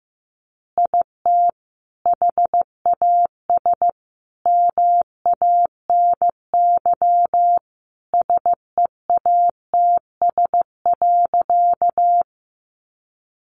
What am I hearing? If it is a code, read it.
Morse code, heard as a keyed tone, about 15 wpm, 700 Hz. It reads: IT HAS MANY SEATS.
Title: Additional Easy Phrases 15wpm